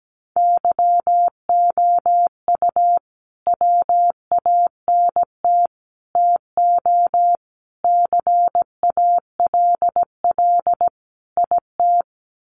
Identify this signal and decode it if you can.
Morse code, heard as a keyed tone, about 17 wpm, 700 Hz. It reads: YOU WANT TO CALL IT
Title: Sets of 5 Words Encore2 17wpm